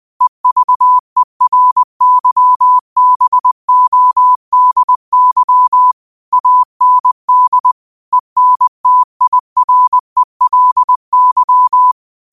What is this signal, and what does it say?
Morse code, heard as a keyed tone, about 20 wpm, 1000 Hz. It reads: EVERYBODY AND ENTIRELY